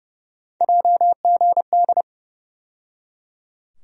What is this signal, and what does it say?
Morse code, heard as a keyed tone, about 32 wpm, 700 Hz. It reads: JGD
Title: USA Call Sign Postfixes - 3 Letters 30wpm